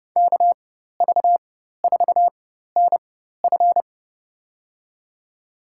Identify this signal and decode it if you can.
Morse code, heard as a keyed tone, about 31 wpm, spaced as for 14 wpm, 700 Hz. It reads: KV4NF
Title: USA Call Signs - 5 Characters - 4x Character Spacing - 30wpm